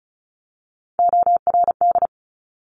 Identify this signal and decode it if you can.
Morse code, heard as a keyed tone, about 36 wpm, 690 Hz. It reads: ORD
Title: Top Three Letter Combinations in Words 35wpm